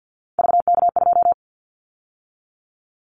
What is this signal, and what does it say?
Morse code, heard as a keyed tone, about 50 wpm, 700 Hz. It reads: VK2
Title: Callsign Prefixes for Top 100 DXCC - 50wpm